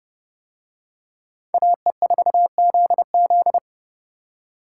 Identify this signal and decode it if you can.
Morse code, heard as a keyed tone, about 29 wpm, 700 Hz. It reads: AE4ZZ